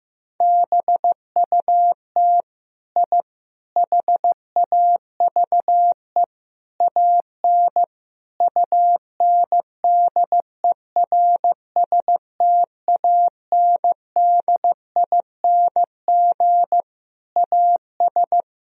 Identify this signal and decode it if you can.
Morse code, heard as a keyed tone, about 15 wpm, 700 Hz. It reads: BUT I HAVE AN UNDERSTANDING AS
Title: Sentences from Top 1000 Words - Review Part 3 - 15wpm